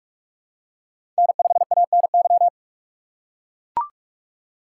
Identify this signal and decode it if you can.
Morse code, heard as a keyed tone, about 44 wpm, 700 Hz. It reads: N5ANY
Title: USA Call Signs - 6x Character Spacing - 45wpm